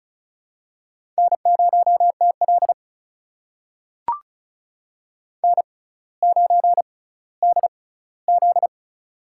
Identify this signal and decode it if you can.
Morse code, heard as a keyed tone, about 35 wpm, spaced as for 18 wpm, 700 Hz. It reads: N0TL N9DZ
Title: USA Call Signs - 4 Characters - 6x Character Spacing - 35wpm